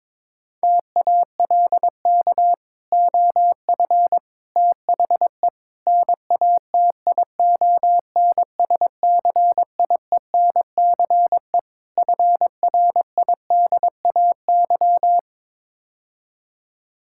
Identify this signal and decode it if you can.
Morse code, heard as a keyed tone, about 22 wpm, 700 Hz. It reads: TALK OF THE NATION SCIENCE FRIDAY
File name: practice / 22wpm / Build A Phrase - 5-Word Phrases - 22wpm